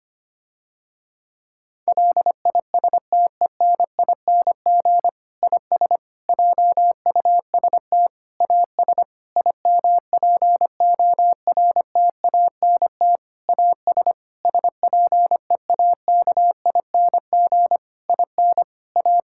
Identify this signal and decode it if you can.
Morse code, heard as a keyed tone, about 25 wpm, 700 Hz. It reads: LISTENING IS JUST AS IMPORTANT AS SPEAKING IN A
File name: practice / 25wpm / Difficult Sentences with Hints - 25wpm